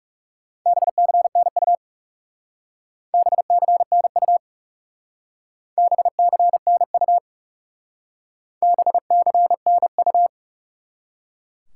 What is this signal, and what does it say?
Morse code, heard as a keyed tone, about 38 wpm, 700 Hz. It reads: BCNU BCNU BCNU BCNU